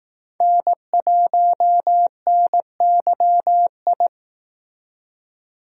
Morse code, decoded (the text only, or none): N1NYI